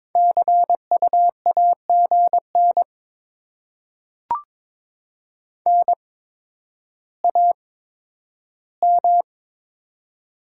Morse code, heard as a keyed tone, about 22 wpm, spaced as for 11 wpm, 700 Hz. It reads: CU AGN NAM
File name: Top 20 QSO Elements - 8x Character Spacing - 22wpm